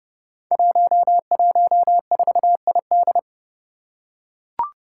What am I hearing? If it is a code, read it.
Morse code, heard as a keyed tone, about 30 wpm, 700 Hz. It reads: 114ID
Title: Five Challenging Letter-Number Combinations 30wpm